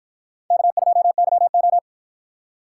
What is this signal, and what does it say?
Morse code, heard as a keyed tone, about 53 wpm, 700 Hz. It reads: K2YO